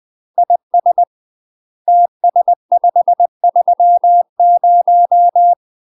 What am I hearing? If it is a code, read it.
Morse code, heard as a keyed tone, about 20 wpm, 700 Hz. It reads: IS TS530
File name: UK MASTER QSOs TRAINING / 20WPM / UK MASTER QSOs 0045